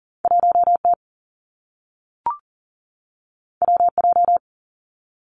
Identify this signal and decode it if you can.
Morse code, heard as a keyed tone, about 40 wpm, 700 Hz. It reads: 1T WJ
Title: Two Challenging Letter-Number Combinations 40wpm